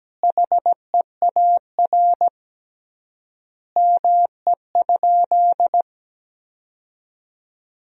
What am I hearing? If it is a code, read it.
Morse code, heard as a keyed tone, about 17 wpm, 700 Hz. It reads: HEAR ME?
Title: 3x Word-Spacing - Sentences from Top 300 Words - 17wpm